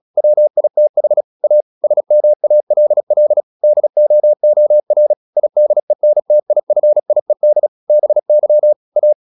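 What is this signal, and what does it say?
Morse code, heard as a keyed tone, about 36 wpm, 600 Hz. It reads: WITH A SMALL DOOR IDENTIFIED BY A
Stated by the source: Morse practice at 36 wpm, text of Ken Follett's novel Never, from CW med Gnister.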